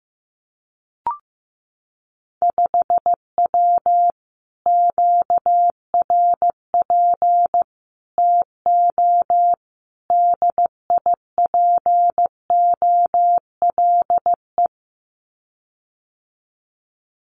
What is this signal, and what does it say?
Morse code, heard as a keyed tone, about 16 wpm, 700 Hz. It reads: 5W QRP TO DIPOLE
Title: Common QSO Phrases Encore 15wpm